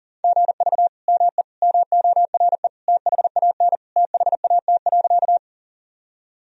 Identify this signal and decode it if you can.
Morse code, heard as a keyed tone, about 40 wpm, 700 Hz. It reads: GV ME MORE THAN THAT.
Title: Sentences from Top 200 Words and Abbreviations 40wpm